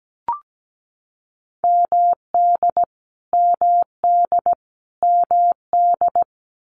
Morse code, heard as a keyed tone, about 17 wpm, 700 Hz. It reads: MD MD MD
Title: US States Repeated 3x 17wpm